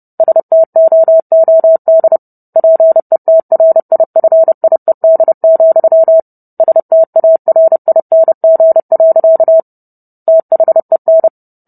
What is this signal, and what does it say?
Morse code, heard as a keyed tone, about 30 wpm, 650 Hz. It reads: STOOD PETRIFIED, STARING. THEN